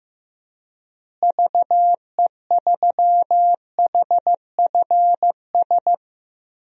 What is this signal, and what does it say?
Morse code, heard as a keyed tone, about 15 wpm, 700 Hz. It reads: VE3HFS
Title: DXCC-100 Call Signs - 6 Characters - Repeated 3x - 15wpm